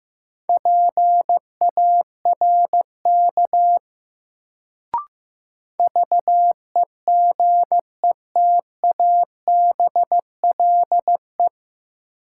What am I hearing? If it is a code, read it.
Morse code, heard as a keyed tone, about 15 wpm, 700 Hz. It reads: PARK VEGETABLE
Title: Sentences from Top 1300 Words - New Word Drill Alt - 15wpm